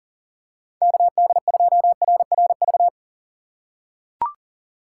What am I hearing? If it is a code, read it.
Morse code, heard as a keyed tone, about 37 wpm, 700 Hz. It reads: KD2RRV
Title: USA Call Signs - 6 Characters - 3x Character Spacing - 40wpm